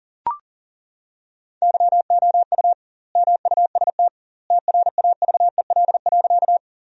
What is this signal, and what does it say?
Morse code, heard as a keyed tone, about 37 wpm, 700 Hz. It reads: YOU MUST TRAVEL.
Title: 2x Word-Spacing - Sentences from Top 400 Words - 40wpm